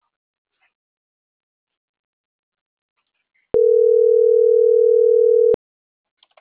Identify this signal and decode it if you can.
Ringback tone (North America), 440 + 480 Hz.